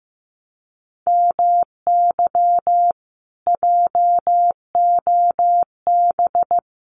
Morse code, heard as a keyed tone, about 15 wpm, 700 Hz. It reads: MY JOB
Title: Common QSO Phrases Encore 15wpm